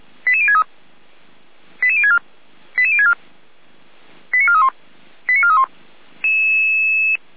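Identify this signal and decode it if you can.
Five-tone selective call (ZVEI), heard as quick runs of five tones, tones of about 80 ms. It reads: 88063 88064 88064 89431 89431
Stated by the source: Received on SDR, ZVEI five-tone code 88166.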